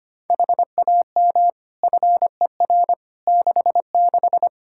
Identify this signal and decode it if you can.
Morse code, heard as a keyed tone, about 25 wpm, 700 Hz. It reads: HAM FER 66
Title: Practice QSO - Exchange 3 - More Info - 25wpm